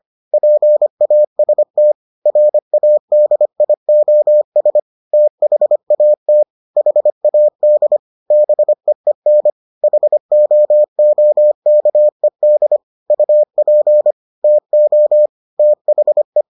Morse code, heard as a keyed tone, about 25 wpm, 600 Hz. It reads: PAST RADIOS THAT HAD BEEN HOOKED UP TO THE